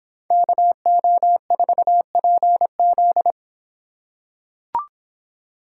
Morse code, heard as a keyed tone, about 25 wpm, 700 Hz. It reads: KO4PZ